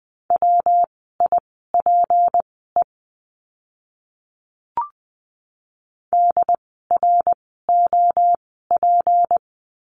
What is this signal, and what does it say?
Morse code, heard as a keyed tone, about 20 wpm, spaced as for 14 wpm, 700 Hz. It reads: WIPE DROP
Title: 153 Lesson 25f - Letter p - Words 20wpm